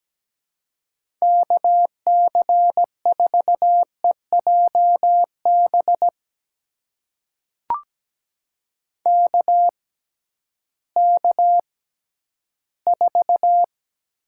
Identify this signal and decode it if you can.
Morse code, heard as a keyed tone, about 17 wpm, spaced as for 11 wpm, 700 Hz. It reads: KC4EJB KK4
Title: USA Call Signs - 6 Characters - 6x Character Spacing - 17wpm